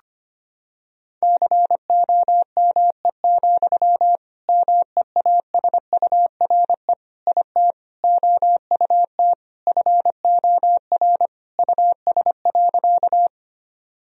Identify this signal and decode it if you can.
Morse code, heard as a keyed tone, about 25 wpm, 700 Hz. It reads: COME, MEASURE IT OUT FOR US.
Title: Sentences from Top 1000 Words - Review Part 2 - 25wpm